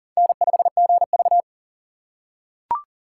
Morse code, heard as a keyed tone, about 37 wpm, 700 Hz. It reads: N5GV
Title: USA Call Signs - 4 Characters - 6x Character Spacing - 40wpm